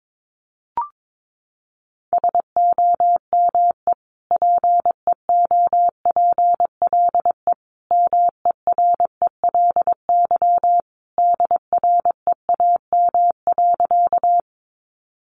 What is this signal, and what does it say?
Morse code, heard as a keyed tone, about 22 wpm, 700 Hz. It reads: SOME PEOPLE MERELY DREAM.
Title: Sentences from Top 1500 Words 22wpm